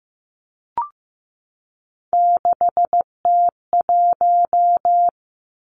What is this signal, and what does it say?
Morse code, heard as a keyed tone, about 15 wpm, 700 Hz. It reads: 6T1
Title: Three Challenging Letter-Number Combinations 15wpm